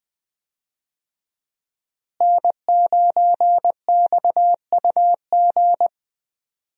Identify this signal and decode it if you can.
Morse code, heard as a keyed tone, about 20 wpm, 700 Hz. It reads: N9XUG
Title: USA Call Signs - 5 Characters - Repeated 3x - 20wpm